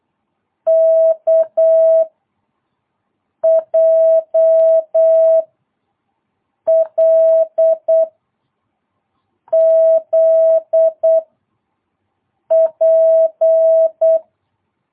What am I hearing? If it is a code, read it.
Morse code, heard as a keyed tone, about 8 wpm, 660 Hz. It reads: K J L Z P
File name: jh8cbh / 2022cw